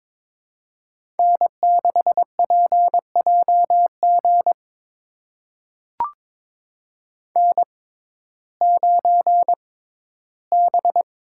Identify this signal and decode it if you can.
Morse code, heard as a keyed tone, about 22 wpm, spaced as for 14 wpm, 700 Hz. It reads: N6PJG N9B
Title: USA Call Signs - 5 Characters - 6x Character Spacing - 22wpm